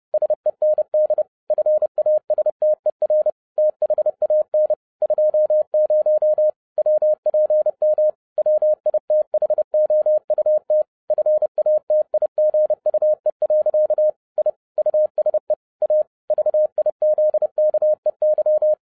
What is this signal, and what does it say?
Morse code, heard as a keyed tone, about 30 wpm, 610 Hz. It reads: SEND FASTER THAN 20 WPM WITHOUT FATIGUE. I USE A VIZKEY
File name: 2011-07-mp3 / n20 GREETINGS FROM SINGAPORE 30wpm